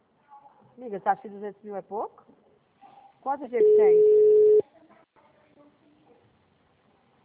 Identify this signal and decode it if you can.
Ringback tone (Europe, 425 Hz), with speech.
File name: out- -2013- - - .62426